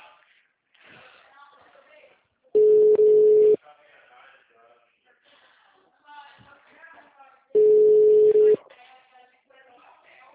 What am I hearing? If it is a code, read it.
Ringback tone (Europe, 425 Hz), with speech.